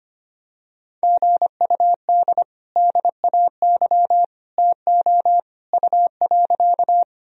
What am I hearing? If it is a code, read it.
Morse code, heard as a keyed tone, about 25 wpm, 700 Hz. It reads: GUD DAY TO U.
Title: Sentences from Top 200 Words and Abbreviations 25wpm